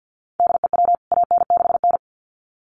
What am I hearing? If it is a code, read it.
Morse code, heard as a keyed tone, about 50 wpm, 700 Hz. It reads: DEW AN6N
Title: CWT Exchange 50wpm